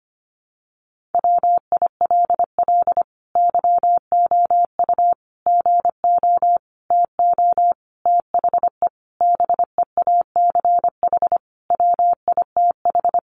Morse code, heard as a keyed tone, about 25 wpm, 700 Hz. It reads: WILL YOU GO TO THE BEACH WITH
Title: Sentences from Top 1500 Words 25wpm